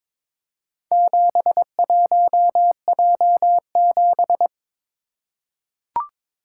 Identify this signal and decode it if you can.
Morse code, heard as a keyed tone, about 23 wpm, 700 Hz. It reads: 71J7
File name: practice / 22wpm / Four Letter-Number Combinations - 22wpm